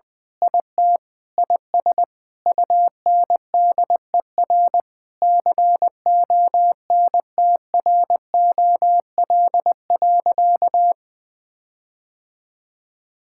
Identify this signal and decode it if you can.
Morse code, heard as a keyed tone, about 21 wpm, 700 Hz. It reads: IT IS UNDER CONTROL.
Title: Sentences with 9-Word Phrases - 20wpm